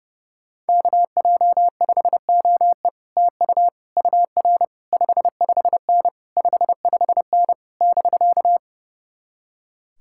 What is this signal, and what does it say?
Morse code, heard as a keyed tone, about 35 wpm, 700 Hz. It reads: KJ5OE TU UR 55N 55N <BK>
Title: Speed-Racing Common QSO Phrases Encore - 30wpm